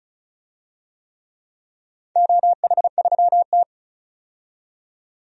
Morse code, heard as a keyed tone, about 36 wpm, 700 Hz. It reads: OH3T
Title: DXCC-100 Call Signs - 4 Characters - 35wpm